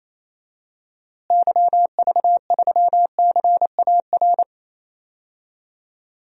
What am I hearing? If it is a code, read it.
Morse code, heard as a keyed tone, about 28 wpm, 700 Hz. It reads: YV3CAR
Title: DXCC-100 Call Signs - 6 Characters - Repeated 2x - 28wpm